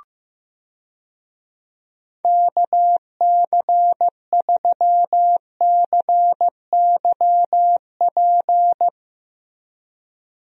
Morse code, heard as a keyed tone, about 15 wpm, 700 Hz. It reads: KC3CYP
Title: USA Call Signs - 6 Characters - Repeated 3x - 15wpm